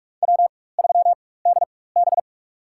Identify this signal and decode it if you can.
Morse code, heard as a keyed tone, about 47 wpm, spaced as for 21 wpm, 700 Hz. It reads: W3DB
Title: USA Call Signs - 4x Character Spacing - 45wpm